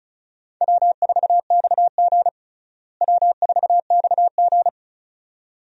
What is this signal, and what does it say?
Morse code, heard as a keyed tone, about 36 wpm, 700 Hz. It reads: W4XG W4XG
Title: USA Call Signs - 4 Characters - Repeated 2x - 35wpm